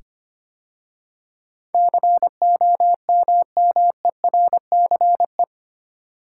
Morse code, heard as a keyed tone, about 25 wpm, 700 Hz. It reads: COMMERCE
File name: 8 Letter Words - First 3 Letters Given - 25wpm